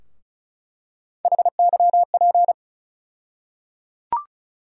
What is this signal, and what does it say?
Morse code, heard as a keyed tone, about 34 wpm, 700 Hz. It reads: HYP
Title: USA Call Sign Postfixes - 3 Letters 35wpm